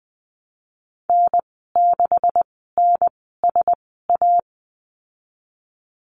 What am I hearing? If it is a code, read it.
Morse code, heard as a keyed tone, about 20 wpm, spaced as for 14 wpm, 700 Hz. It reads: N6NSA
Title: 161 Lesson 26g - Letter g - Call Signs 20wpm